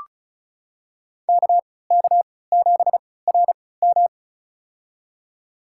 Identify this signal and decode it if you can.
Morse code, heard as a keyed tone, about 36 wpm, spaced as for 20 wpm, 700 Hz. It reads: KK7RM
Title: USA Call Signs - 5 Characters - 3x Character Spacing - 35wpm